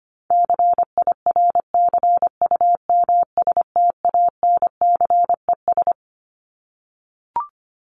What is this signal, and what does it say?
Morse code, heard as a keyed tone, about 25 wpm, 700 Hz. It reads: CIRCUMSTANCES
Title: ES:S Suffix - Top 100 Words with Base Words 25wpm